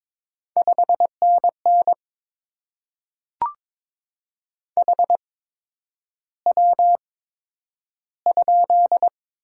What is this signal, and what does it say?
Morse code, heard as a keyed tone, about 22 wpm, spaced as for 9 wpm, 700 Hz. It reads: 5NN HW?
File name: Top 20 QSO Elements - 8x Character Spacing - 22wpm